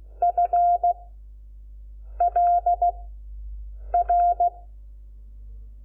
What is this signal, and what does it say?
Morse code, heard as a keyed tone, about 16 wpm, 700 Hz. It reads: F L R
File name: Lecciones / CLASE Nº5